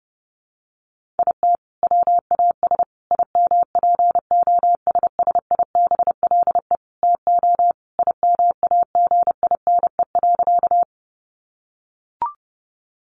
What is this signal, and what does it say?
Morse code, heard as a keyed tone, about 30 wpm, 700 Hz. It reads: IT WAS IMPOSSIBLE TO IMAGINE.
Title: Sentences from Top 1400 Words 30wpm